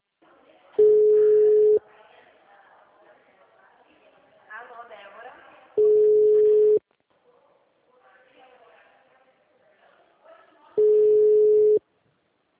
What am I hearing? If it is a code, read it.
Ringback tone (Europe, 425 Hz), with speech.